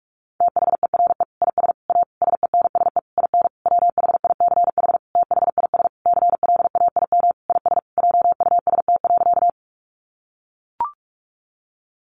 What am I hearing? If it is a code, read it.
Morse code, heard as a keyed tone, about 44 wpm, 700 Hz. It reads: THERE IS A SENSE IN WHICH THIS CLAIM IS JUST.
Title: Sentences from Top 1500 Words 45wpm